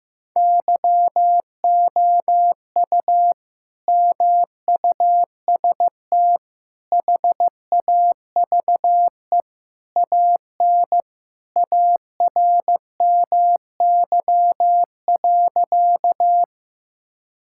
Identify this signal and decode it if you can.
Morse code, heard as a keyed tone, about 16 wpm, 700 Hz. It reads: YOU MUST HAVE AN ARMY.
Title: Sentences from Top 1000 Words - Review Part 5 - 15wpm